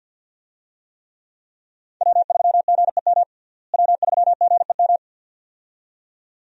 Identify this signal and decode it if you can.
Morse code, heard as a keyed tone, about 52 wpm, 700 Hz. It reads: W3GEM W3GEM
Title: USA Call Signs - 5 Characters - Repeated 2x - 50wpm